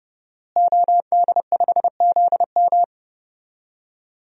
Morse code, heard as a keyed tone, about 32 wpm, 700 Hz. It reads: OD5ZM
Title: DXCC-100 Call Signs - 5 Characters - Repeated 3x - 30wpm